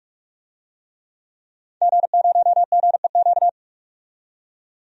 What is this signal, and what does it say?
Morse code, heard as a keyed tone, about 46 wpm, 700 Hz. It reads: G0GEQ
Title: DXCC-100 Call Signs - 5 Characters - 45wpm